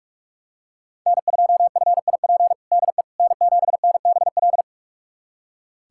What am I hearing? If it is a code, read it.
Morse code, heard as a keyed tone, about 46 wpm, 700 Hz. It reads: N2FIP DE N7NBL
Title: USA Callsign DE Callsign - 5 Characters - 45wpm